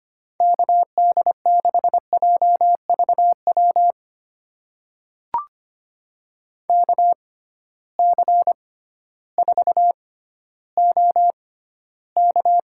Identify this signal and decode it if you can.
Morse code, heard as a keyed tone, about 25 wpm, spaced as for 14 wpm, 700 Hz. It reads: KD6JVW KC4OK
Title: USA Call Signs - 6x Character Spacing - 25wpm